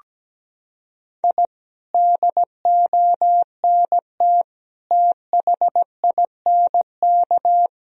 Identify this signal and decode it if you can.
Morse code, heard as a keyed tone, about 17 wpm, 700 Hz. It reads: I DONT THINK
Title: Sentences from Top 1000 Words - Review Part 4 - 17wpm